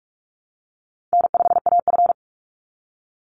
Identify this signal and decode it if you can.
Morse code, heard as a keyed tone, about 44 wpm, 700 Hz. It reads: N5AF
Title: Easy Call Signs Repeated 3x - 45wpm